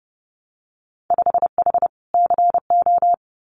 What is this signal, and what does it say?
Morse code, heard as a keyed tone, about 30 wpm, 700 Hz. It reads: 5H CO
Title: Winter Field Day Exchanges Encore 30wpm